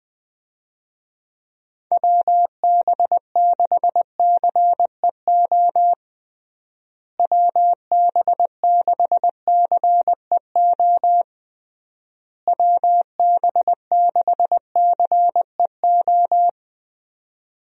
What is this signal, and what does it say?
Morse code, heard as a keyed tone, about 21 wpm, 700 Hz. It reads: WB6CEO WB6CEO WB6CEO
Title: USA Call Signs - 6 Characters - Repeated 3x - 20wpm